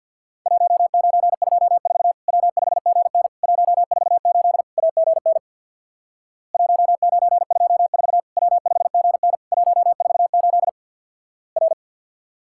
Morse code, heard as a keyed tone, about 50 wpm, 700 Hz. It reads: 1924 W5GN 148 AGN 1924 W5GN 148 R
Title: Worked All Europe DX Contest - QTC Exchange with Repeats - 50wpm